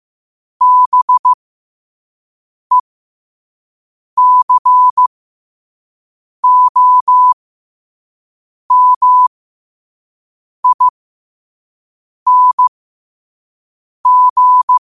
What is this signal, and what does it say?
Morse code, heard as a keyed tone, about 15 wpm, spaced as for 5 wpm, 1000 Hz. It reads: BECOMING
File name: qotdfiles / 5wpm